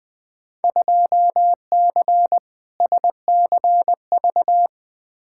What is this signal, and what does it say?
Morse code, heard as a keyed tone, about 20 wpm, 700 Hz. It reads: 2C SCV
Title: Field Day Exchanges Encore - 20wpm